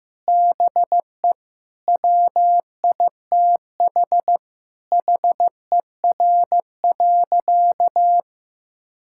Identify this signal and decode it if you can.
Morse code, heard as a keyed tone, about 15 wpm, 700 Hz. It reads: BE WITH HER.
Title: Sentences from Top 1000 Words - Review Part 6 - 15wpm